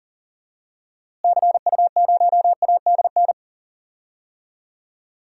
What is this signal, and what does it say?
Morse code, heard as a keyed tone, about 40 wpm, 700 Hz. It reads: CU0ADN
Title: DXCC-100 Call Signs - 6 Characters - Repeated 3x - 40wpm